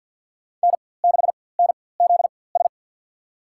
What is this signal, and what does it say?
Morse code, heard as a keyed tone, about 54 wpm, spaced as for 24 wpm, 700 Hz. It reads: N6NZS
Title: USA Call Signs - 4x Character Spacing - 50wpm